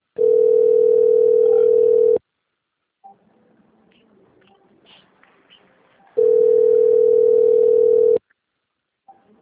Ringback tone (North America), 440 + 480 Hz.
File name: rg-1600-unknown- - - .11654